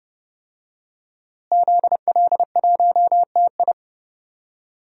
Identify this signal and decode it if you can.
Morse code, heard as a keyed tone, about 30 wpm, 700 Hz. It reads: ZL1TI
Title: DXCC-100 Call Signs - 5 Characters - Repeated 2x - 30wpm